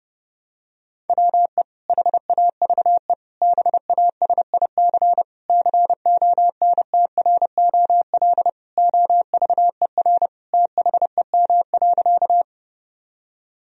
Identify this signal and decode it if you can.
Morse code, heard as a keyed tone, about 29 wpm, 700 Hz. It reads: WE HAVE BASIC CONTROL OVER THEM.
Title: Sentences from Top 1000 Words - Review Part 3 - 30wpm